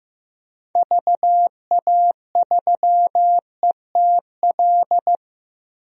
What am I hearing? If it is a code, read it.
Morse code, heard as a keyed tone, about 15 wpm, 700 Hz. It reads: VA3ETL
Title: DXCC-100 Call Signs - 6 Characters - 15wpm